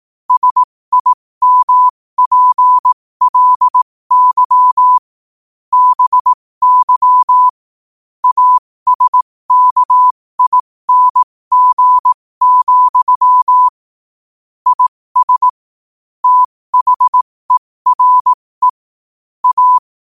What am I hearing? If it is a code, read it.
Morse code, heard as a keyed tone, about 18 wpm, spaced as for 15 wpm, 1000 Hz. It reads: SIMPLY BY ASKING, IS THERE A
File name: qotdfiles / 15wpm